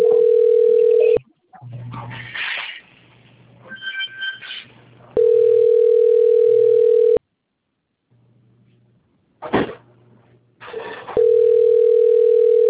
Ringback tone (North America), 440 + 480 Hz.